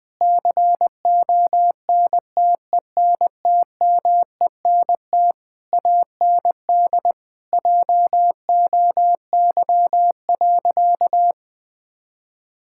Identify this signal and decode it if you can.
Morse code, heard as a keyed tone, about 20 wpm, 700 Hz. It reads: CONTENTMENT AND JOY.
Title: Difficult Sentences with Hints - 20wpm